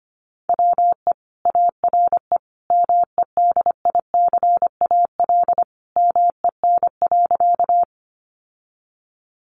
Morse code, heard as a keyed tone, about 25 wpm, 700 Hz. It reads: WE ARE MEDICAL MEN.
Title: Sentences from Top 800 Words 25wpm